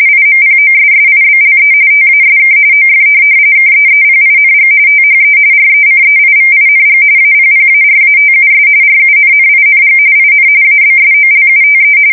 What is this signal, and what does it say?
RTTY, about 45.45 baud. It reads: IMPOSSIBLE TASK OF KEEPING THE RECEIVING APPARATUS RUNNING IN ABSOLUTE
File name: fsk170 / printing-telegraph